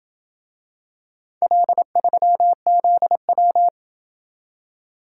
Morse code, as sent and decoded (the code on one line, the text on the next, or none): .-.. ...-- --.. .--
L3ZW